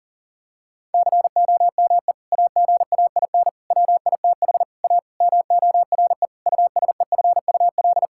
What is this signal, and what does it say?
Morse code, heard as a keyed tone, about 40 wpm, 700 Hz. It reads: COME AGAIN WITH A MORE USEFUL
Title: Sentences from Top 1000 Words - Review Part 6 - 40wpm